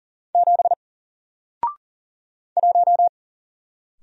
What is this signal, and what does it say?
Morse code, heard as a keyed tone, about 40 wpm, 700 Hz. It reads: 7 1
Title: Single Letter-Number Alt 40wpm